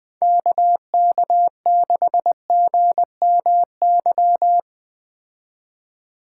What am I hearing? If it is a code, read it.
Morse code, heard as a keyed tone, about 20 wpm, 700 Hz. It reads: KK6GMY